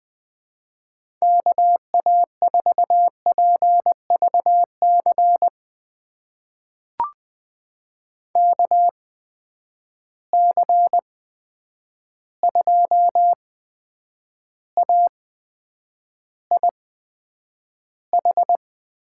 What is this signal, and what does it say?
Morse code, heard as a keyed tone, about 20 wpm, spaced as for 9 wpm, 700 Hz. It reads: KA4PVC KC2AIH
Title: USA Call Signs - 8x Character Spacing - 20wpm